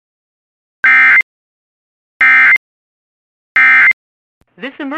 Emergency Alert System (SAME) tones, heard as harsh buzzing bursts. It is heard from 0.8 to 3.9 s.